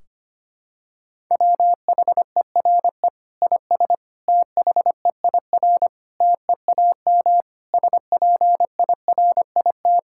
Morse code, heard as a keyed tone, about 25 wpm, 700 Hz. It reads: WHERE IS THEIR TEAM SPIRIT
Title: Sentences from Top 1600 Words 25wpm